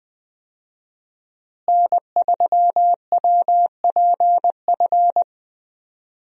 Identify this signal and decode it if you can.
Morse code, heard as a keyed tone, about 20 wpm, 700 Hz. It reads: N3WPF
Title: USA Call Signs - 5 Characters - Repeated 3x - 20wpm